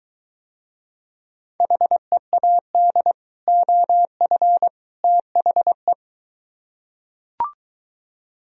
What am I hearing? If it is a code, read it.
Morse code, heard as a keyed tone, about 23 wpm, 700 Hz. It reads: HEAD OF THE